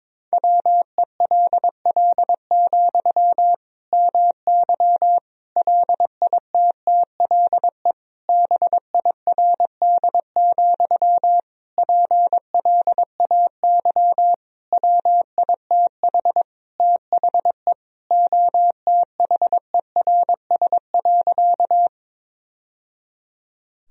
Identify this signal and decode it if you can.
Morse code, heard as a keyed tone, about 22 wpm, 700 Hz. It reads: WELL, MY LITTLE BIRD, PLAY WITH THE OTHERS.
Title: Sentences from Top 1000 Words - Review Part 2 - 22wpm